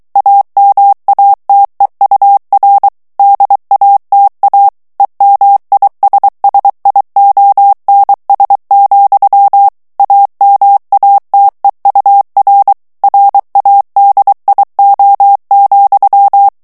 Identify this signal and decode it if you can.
Morse code, heard as a keyed tone, about 23 wpm, 800 Hz. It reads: AMATEUR DATA EMISSIONS, AMATEUR RADIO,
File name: ANGLAIS texte clair 3 vitesse 22 mots minute